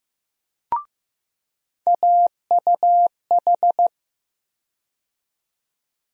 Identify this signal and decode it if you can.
Morse code, heard as a keyed tone, about 15 wpm, 700 Hz. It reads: AUH
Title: USA Call Sign Postfixes - 3 Letters 15wpm